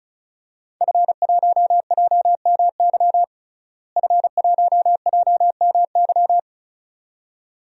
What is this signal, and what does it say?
Morse code, heard as a keyed tone, about 34 wpm, 700 Hz. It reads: F1JMY F1JMY
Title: DXCC-100 Call Signs - 5 Characters - Repeated 2x - 35wpm